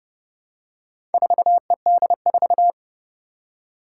Morse code, heard as a keyed tone, about 30 wpm, 700 Hz. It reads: 4ED4
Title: Four Challenging Letter-Number Combinations 30wpm